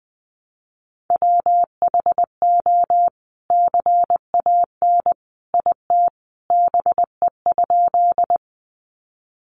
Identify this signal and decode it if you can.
Morse code, heard as a keyed tone, about 20 wpm, 700 Hz. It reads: WHO CAN IT BE?
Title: Sentences from Top 100 Words 20wpm